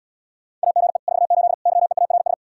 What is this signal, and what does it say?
Morse code, heard as a keyed tone, about 80 wpm, 700 Hz. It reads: AGE 48 YEARS